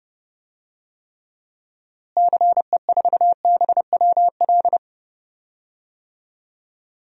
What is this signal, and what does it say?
Morse code, heard as a keyed tone, about 30 wpm, 700 Hz. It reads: CE4BWL
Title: DXCC-100 Call Signs - 6 Characters - Repeated 2x - 30wpm